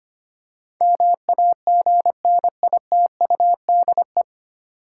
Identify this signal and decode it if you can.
Morse code, heard as a keyed tone, about 25 wpm, 690 Hz. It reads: MAGNITUDE
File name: practice / 25wpm / 9 Letter Words - First 3 Letters Given - 25wpm